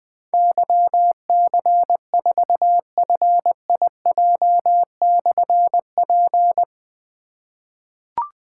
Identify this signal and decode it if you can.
Morse code, heard as a keyed tone, about 20 wpm, 700 Hz. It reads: YC4FIJ/P
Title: Difficult Call Signs 20wpm